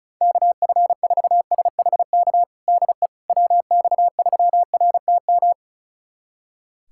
Morse code, heard as a keyed tone, about 36 wpm, 700 Hz. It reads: KF4SHK DE WX3RTM
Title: USA Callsign DE Callsign - 6 Characters - 35wpm